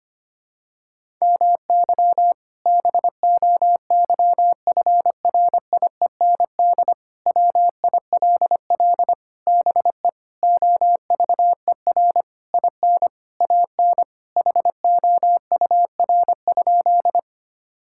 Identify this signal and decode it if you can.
Morse code, heard as a keyed tone, about 25 wpm, 700 Hz. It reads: MY BOYFRIEND WILL BE OVER IN AN HOUR?
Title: Sentences from Top 1000 Words - Review Part 5 - 25wpm